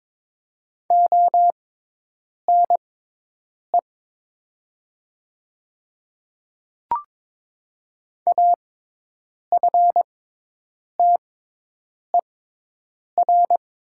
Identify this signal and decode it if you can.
Morse code, heard as a keyed tone, about 22 wpm, spaced as for 8 wpm, 700 Hz. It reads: ONE AFTER
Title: 6x Character Spacing - Top 300 Words - 22wpm